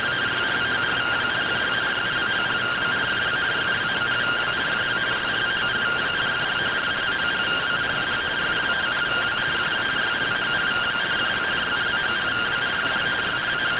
Two-tone teleprinter signal, about 170 Hz shift.
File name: findings / SAGITAL fr15 nb203 3mm.gen-21 rtty rx